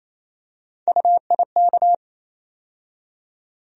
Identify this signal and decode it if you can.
Morse code, heard as a keyed tone, about 28 wpm, 700 Hz. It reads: UIK
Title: USA Call Sign Postfixes - 3 Letters 28wpm